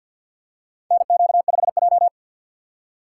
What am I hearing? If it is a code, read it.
Morse code, heard as a keyed tone, about 50 wpm, 700 Hz. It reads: NQ5J
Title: USA Call Signs - 4 Characters - 2x Character Spacing - 50wpm